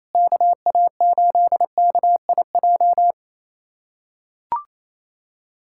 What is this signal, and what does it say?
Morse code, heard as a keyed tone, about 28 wpm, 700 Hz. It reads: KA8KIJ
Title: USA Callsign DE Callsign - 6 Characters - 28wpm